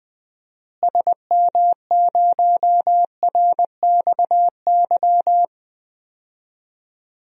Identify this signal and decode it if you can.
Morse code, heard as a keyed tone, about 20 wpm, 700 Hz. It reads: SM0RXY
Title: DXCC-100 Call Signs - 6 Characters - Repeated 3x - 20wpm